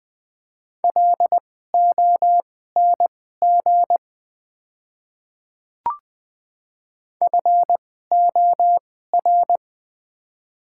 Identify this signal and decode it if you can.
Morse code, heard as a keyed tone, about 20 wpm, spaced as for 14 wpm, 700 Hz. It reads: LONG FOR
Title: 2x Character Spacing - Top 200 Words - 20wpm